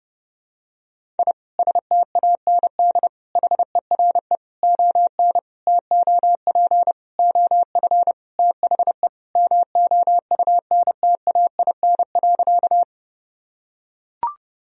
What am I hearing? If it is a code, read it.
Morse code, heard as a keyed tone, about 30 wpm, 700 Hz. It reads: I STAND HERE ON TOP OF THE MOUNTAIN.
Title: Sentences from Top 1000 Words - Review Part 1 - 30wpm